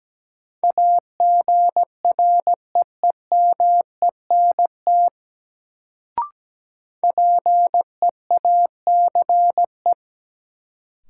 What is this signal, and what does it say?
Morse code, heard as a keyed tone, about 17 wpm, 700 Hz. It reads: AGREEMENT PEACE
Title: Sentences from Top 1500 Words - New Word Drill - 17wpm